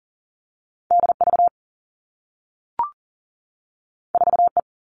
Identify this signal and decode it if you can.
Morse code, heard as a keyed tone, about 40 wpm, 700 Hz. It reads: D V 4 E
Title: Two Challenging Letter-Number Combinations 40wpm